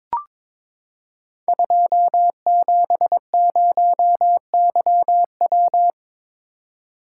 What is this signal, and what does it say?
Morse code, heard as a keyed tone, about 23 wpm, 700 Hz. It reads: 270YW
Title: Code Groups - 5 Letter:Numbers - 22wpm